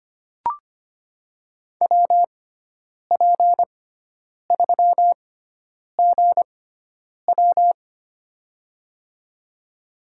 Morse code, heard as a keyed tone, about 25 wpm, spaced as for 9 wpm, 700 Hz. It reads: WP3GW
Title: USA Call Signs - 5 Characters - 6x Character Spacing - 25wpm